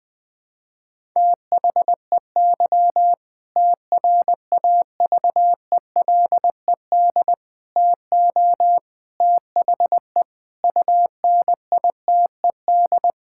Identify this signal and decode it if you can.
Morse code, heard as a keyed tone, about 21 wpm, 700 Hz. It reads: THEY TRAVELED TO THE UNITED
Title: Sentences with 2-Word Phrases - 20wpm